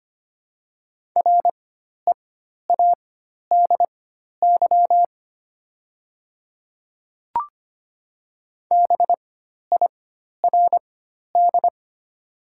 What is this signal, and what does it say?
Morse code, heard as a keyed tone, about 25 wpm, spaced as for 12 wpm, 700 Hz. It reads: READY BIRD
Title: 4x Character Spacing - Top 500 Words - 25wpm